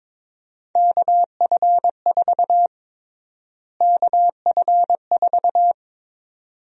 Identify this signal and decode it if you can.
Morse code, heard as a keyed tone, about 22 wpm, 700 Hz. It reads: KF4 KF4
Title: USA Call Sign Prefixes - Repeated 2x - 22wpm